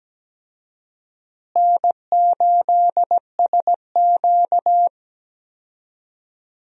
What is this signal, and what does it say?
Morse code, heard as a keyed tone, about 17 wpm, 700 Hz. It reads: N8SQ
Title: USA Call Signs - 4 Characters - Repeated 2x - 17wpm